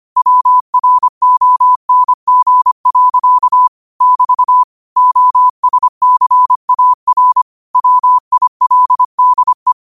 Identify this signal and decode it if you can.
Morse code, heard as a keyed tone, about 25 wpm, 1000 Hz. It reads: WRONG. <BT> OSCAR WILDE
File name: qotdfiles / 25wpm